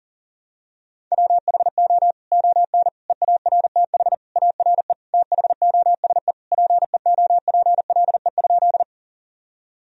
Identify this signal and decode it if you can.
Morse code, heard as a keyed tone, about 40 wpm, 700 Hz. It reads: WHO ON EARTH ARE THOSE PEOPLE?
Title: Sentences from Top 1000 Words - Review Part 1 - 40wpm